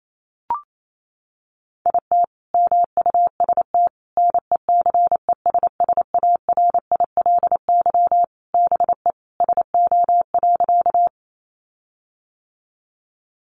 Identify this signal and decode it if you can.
Morse code, heard as a keyed tone, about 28 wpm, 700 Hz. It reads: IT MUST NECESSARILY BE SO.
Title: Sentences from Top 1400 Words 28wpm